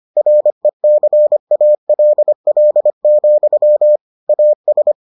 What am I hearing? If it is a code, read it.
Morse code, heard as a keyed tone, about 25 wpm, 600 Hz. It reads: RECALL, AS